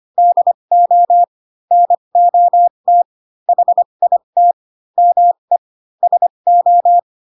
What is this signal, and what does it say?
Morse code, heard as a keyed tone, about 25 wpm, spaced as for 22 wpm, 700 Hz. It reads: DO NOT HIT ME SO